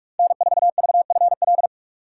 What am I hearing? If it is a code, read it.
Morse code, heard as a keyed tone, about 46 wpm, 700 Hz. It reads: N4VFL
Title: USA Call Signs - 5 Characters - 4x Character Spacing - 45wpm